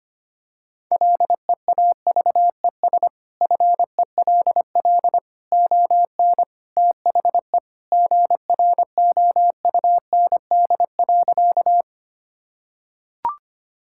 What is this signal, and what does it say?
Morse code, heard as a keyed tone, about 25 wpm, 700 Hz. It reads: LEAVES FELL ON THE GROUND.
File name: practice / 25wpm / Sentences with 3-Word Phrases - 25wpm